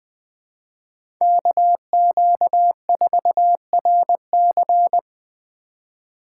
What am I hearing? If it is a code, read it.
Morse code, heard as a keyed tone, about 20 wpm, 700 Hz. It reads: KQ4RC
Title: USA Call Signs - 5 Characters - 4x Character Spacing - 20wpm